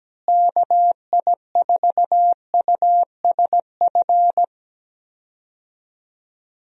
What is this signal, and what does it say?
Morse code, heard as a keyed tone, about 17 wpm, 700 Hz. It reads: KI4USF
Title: USA Call Signs - 6 Characters - Repeated 2x - 17wpm